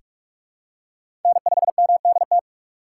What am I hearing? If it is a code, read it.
Morse code, heard as a keyed tone, about 45 wpm, 700 Hz. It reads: N5MDT
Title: USA Call Signs - 2x Character Spacing - 45wpm